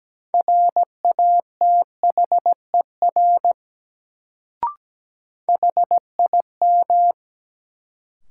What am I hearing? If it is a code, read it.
Morse code, heard as a keyed tone, about 17 wpm, 700 Hz. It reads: RATHER HIM
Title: Sets of 2 Words - New Word Drill - 17wpm